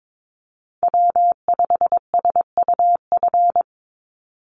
Morse code, heard as a keyed tone, about 22 wpm, 700 Hz. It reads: W5SUF
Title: USA Call Signs 22wpm